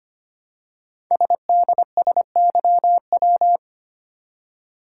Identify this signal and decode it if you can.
Morse code, heard as a keyed tone, about 25 wpm, 700 Hz. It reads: SDSYW
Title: Code Groups - 5 Letter:Numbers - 25wpm